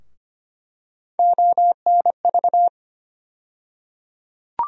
Morse code, heard as a keyed tone, about 25 wpm, 700 Hz. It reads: ONV
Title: USA Call Sign Postfixes - 3 Letters 25wpm